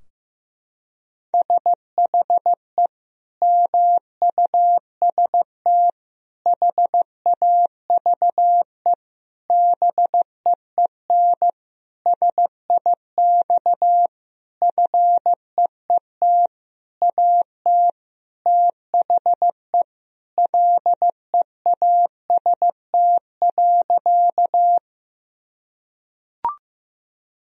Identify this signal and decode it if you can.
Morse code, heard as a keyed tone, about 15 wpm, 700 Hz. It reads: SHE MUST HAVE BEEN SIX FEET AT THE LEAST.
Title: Sentences from Top 1000 Words - Review Part 2 - 15wpm